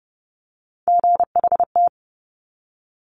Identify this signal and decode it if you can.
Morse code, heard as a keyed tone, about 30 wpm, 690 Hz. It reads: GHT
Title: Top Three Letter Combinations in Words 30wpm